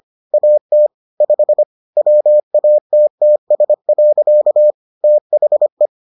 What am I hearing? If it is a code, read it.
Morse code, heard as a keyed tone, about 25 wpm, 600 Hz. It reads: AT 5 WATTS. THE